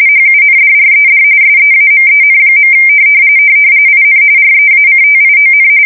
RTTY, about 45.45 baud. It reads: HUNDRED SEAMEN. IN REACTION
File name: fsk170 / usnavy1